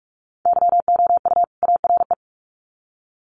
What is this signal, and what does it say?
Morse code, heard as a keyed tone, about 44 wpm, 700 Hz. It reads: YOU ARE
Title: Sets of 2 Words 45wpm